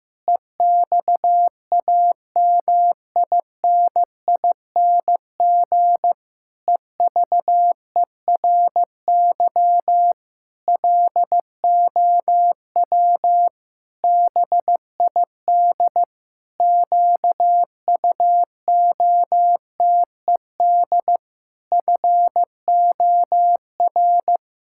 Morse code, heard as a keyed tone, about 15 wpm, 700 Hz. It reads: EXAMINING EVERY LOW BID QUOTED FOR
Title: Pangram Challenge v2 - 15wpm